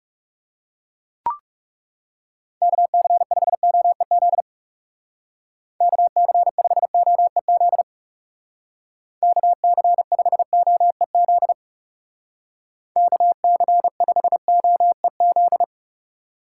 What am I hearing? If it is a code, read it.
Morse code, heard as a keyed tone, about 39 wpm, 700 Hz. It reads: KC5OEZ KC5OEZ KC5OEZ KC5OEZ